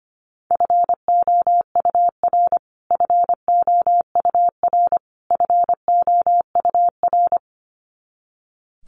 Morse code, heard as a keyed tone, about 25 wpm, 700 Hz. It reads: FOUR FOUR FOUR
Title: Top 500 Words Repeated 25wpm